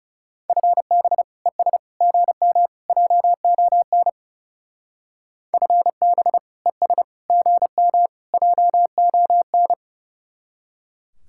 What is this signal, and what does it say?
Morse code, heard as a keyed tone, about 36 wpm, 700 Hz. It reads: FB ES GM JON FB ES GM JON
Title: Speed-Racing Common QSO Phrases - 30wpm